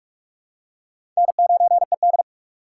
Morse code, heard as a keyed tone, about 46 wpm, 700 Hz. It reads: N9ED
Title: USA Call Signs - 4 Characters - 4x Character Spacing - 45wpm